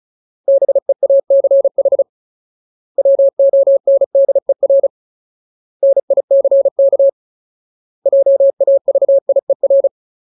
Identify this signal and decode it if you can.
Morse code, heard as a keyed tone, about 34 wpm, 550 Hz. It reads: BEACH WONDER NICK JAVIER
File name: Ultimate Word List / 35WPM Ultimate